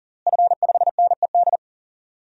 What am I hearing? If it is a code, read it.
Morse code, heard as a keyed tone, about 40 wpm, 700 Hz. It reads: F5NED
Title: DXCC-100 Call Signs - 5 Characters - Repeated 3x - 40wpm